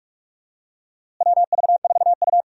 Morse code, heard as a keyed tone, about 46 wpm, 700 Hz. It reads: WV4U
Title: USA Call Signs - 4 Characters - Repeated 2x - 45wpm